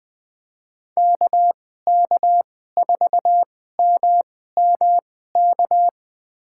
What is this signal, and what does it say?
Morse code, heard as a keyed tone, about 20 wpm, spaced as for 14 wpm, 700 Hz. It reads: KK4MMK
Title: USA Call Signs - 2x Character Spacing - 20wpm